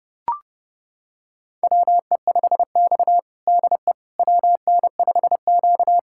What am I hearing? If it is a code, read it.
Morse code, heard as a keyed tone, about 29 wpm, 700 Hz. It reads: WE5X DE WN5Q
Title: USA Callsign DE Callsign - 4 Characters - 30wpm